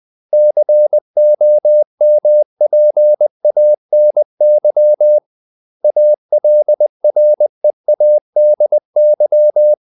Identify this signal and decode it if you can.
Morse code, heard as a keyed tone, about 20 wpm, 600 Hz. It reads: COMPANY ALREADY
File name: CW / Practice / 500MostCommonEnglishWords20